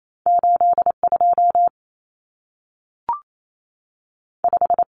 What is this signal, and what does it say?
Morse code, heard as a keyed tone, about 28 wpm, 700 Hz. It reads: 82 5
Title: Common Cut Numbers - 2-Digit 28wpm